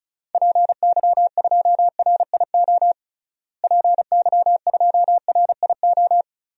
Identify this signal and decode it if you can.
Morse code, heard as a keyed tone, about 34 wpm, 700 Hz. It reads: PY2RIO PY2RIO
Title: DXCC-100 Call Signs - 6 Characters - Repeated 3x - 35wpm